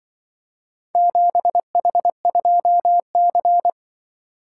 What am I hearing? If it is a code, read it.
Morse code, heard as a keyed tone, about 24 wpm, 700 Hz. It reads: 7H2C
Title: Speed-Racing - Four Letter-Number Combinations - 20wpm